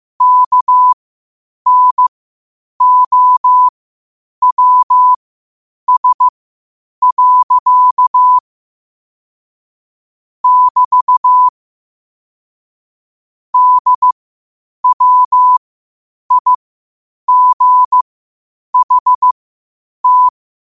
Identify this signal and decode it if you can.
Morse code, heard as a keyed tone, about 15 wpm, spaced as for 8 wpm, 1000 Hz. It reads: KNOWS. <BT> DWIGHT
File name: qotdfiles / 8wpm